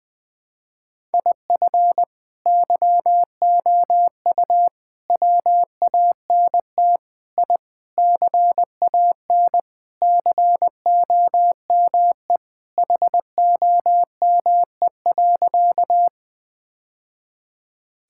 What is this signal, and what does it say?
Morse code, heard as a keyed tone, about 20 wpm, 700 Hz. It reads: IF YOU WANT I CAN COME HOME.
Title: Sentences with 3-Word Phrases - 20wpm